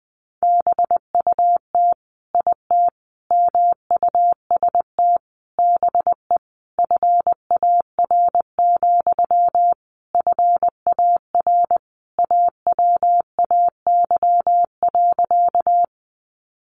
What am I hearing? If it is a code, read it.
Morse code, heard as a keyed tone, about 20 wpm, 700 Hz. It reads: BUT IT MUST BE FAR, FAR AWAY.
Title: Sentences from Top 600 Words 20wpm